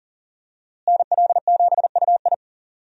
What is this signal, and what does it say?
Morse code, heard as a keyed tone, about 40 wpm, 700 Hz. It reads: NL7UI
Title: USA Call Signs - 5 Characters - Repeated 2x - 40wpm